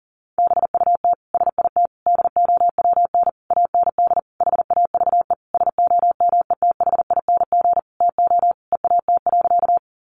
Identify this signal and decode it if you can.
Morse code, heard as a keyed tone, about 40 wpm, 700 Hz. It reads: BUT SIT DOWN AND HAVE SOMETHING TO EAT.
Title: Sentences from Top 600 Words 40wpm